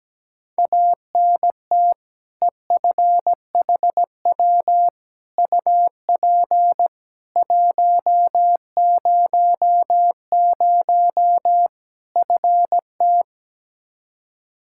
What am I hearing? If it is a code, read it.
Morse code, heard as a keyed tone, about 17 wpm, 700 Hz. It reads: ANT EFHW UP 100 FT
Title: Practice QSO - Exchange 2 - Info - 17wpm